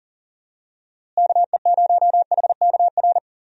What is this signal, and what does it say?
Morse code, heard as a keyed tone, about 40 wpm, 700 Hz. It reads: KE0HKR
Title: USA Call Signs - 6 Characters - Repeated 3x - 40wpm